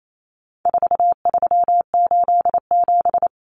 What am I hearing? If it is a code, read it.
Morse code, heard as a keyed tone, about 28 wpm, 700 Hz. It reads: 4387
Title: Numbers - 4-Digit 28wpm